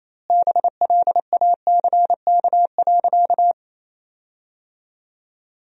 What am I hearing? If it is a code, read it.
Morse code, heard as a keyed tone, about 28 wpm, 700 Hz. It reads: BLACK.
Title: 2x Word-Spacing - Sentences from Top 400 Words - 28wpm